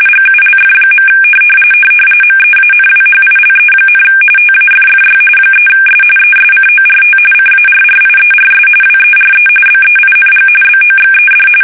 RTTY, about 45.45 baud. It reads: GULF. MEANWHILE, TAFFY 3S ESCORT CARRIERS WERE BEING BOXED
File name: fsk850 / battleshipnj